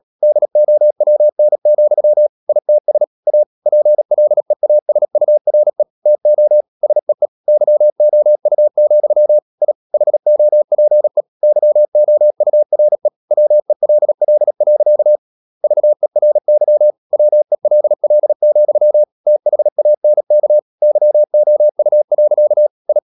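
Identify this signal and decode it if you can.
Morse code, heard as a keyed tone, about 37 wpm, 600 Hz. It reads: DOWN, ITS A PLEASURE TO SEE YOU, I HOPE YOURE WELL. VERY WELL, THANK YOU. I
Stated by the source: Morse code practice at 35–39 wpm sending English text of Ken Follett's novel Never, from CW med Gnister.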